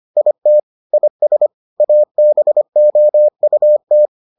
Morse code, heard as a keyed tone, about 25 wpm, 600 Hz. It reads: IT IS ABOUT